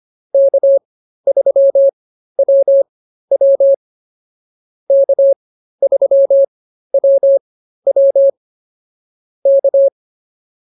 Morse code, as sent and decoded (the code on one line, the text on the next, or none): -.- ...-- .-- .-- / -.- ...-- .-- .-- / -.-
K3WW K3WW K